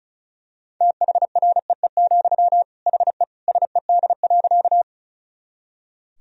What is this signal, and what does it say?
Morse code, heard as a keyed tone, about 34 wpm, 700 Hz. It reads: THREE, HE SED.
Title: Sentences from Top 200 Words and Abbreviations 35wpm